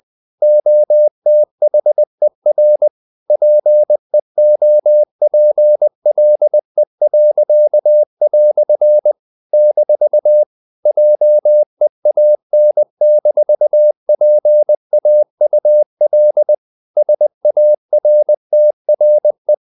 Morse code, heard as a keyed tone, about 20 wpm, 600 Hz. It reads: OTHER PEOPLE." - JEAN-PAUL SARTRE
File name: CW / Practice / EnglishQuotations0320